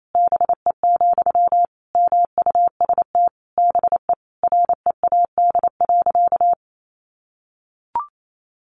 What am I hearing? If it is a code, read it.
Morse code, heard as a keyed tone, about 28 wpm, 700 Hz. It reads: BE, MUST BE READ.
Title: Sentences from Top 1100 Words 28wpm